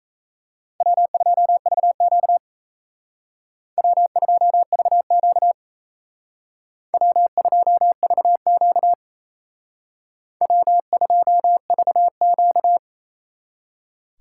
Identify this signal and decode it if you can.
Morse code, heard as a keyed tone, about 34 wpm, 700 Hz. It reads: W2VQ W2VQ W2VQ W2VQ
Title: Speed Racing - USA Call Signs 28wpm